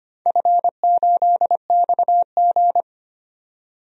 Morse code, heard as a keyed tone, about 25 wpm, 700 Hz. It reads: F8XG
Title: DXCC-100 Call Signs - 4 Characters - 25wpm